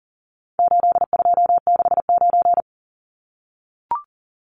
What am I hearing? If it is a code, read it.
Morse code, heard as a keyed tone, about 40 wpm, 700 Hz. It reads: 8269
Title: Numbers - 4-Digit 40wpm